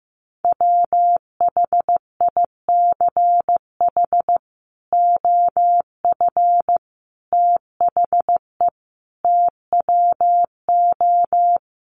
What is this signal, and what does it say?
Morse code, heard as a keyed tone, about 15 wpm, 700 Hz. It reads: WHICH OF THE TWO
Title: Sentences from Top 1300 Words 15wpm